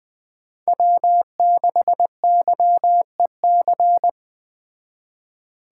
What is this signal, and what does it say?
Morse code, heard as a keyed tone, about 20 wpm, 700 Hz. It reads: W6YEC